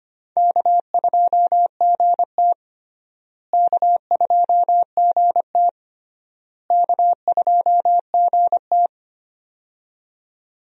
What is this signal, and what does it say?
Morse code, heard as a keyed tone, about 25 wpm, 700 Hz. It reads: K2GT K2GT K2GT
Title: USA Call Signs - 4 Characters - Repeated 3x - 25wpm